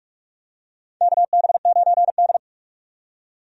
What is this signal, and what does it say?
Morse code, heard as a keyed tone, about 46 wpm, 700 Hz. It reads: KB9D
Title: USA Call Signs - 4 Characters - 45wpm